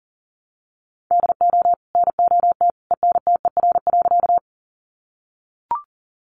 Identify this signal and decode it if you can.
Morse code, heard as a keyed tone, about 40 wpm, 700 Hz. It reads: DO NOT ENTER.
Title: Sentences from Top 1300 Words 40wpm